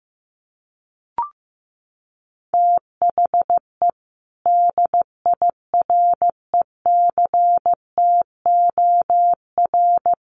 Morse code, heard as a keyed tone, about 15 wpm, 700 Hz. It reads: THE DIRECTOR
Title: Sentences from Top 1300 Words 15wpm